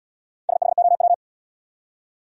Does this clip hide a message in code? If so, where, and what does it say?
Yes: Morse code from 0.5 to 1.1 s, heard as a keyed tone, about 67 wpm, 700 Hz. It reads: SHOW